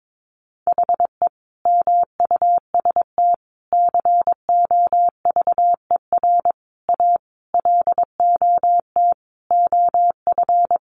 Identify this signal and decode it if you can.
Morse code, heard as a keyed tone, about 22 wpm, 700 Hz. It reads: HE MUST COVER A LOT OF
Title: Sentences from Top 500 Words 22wpm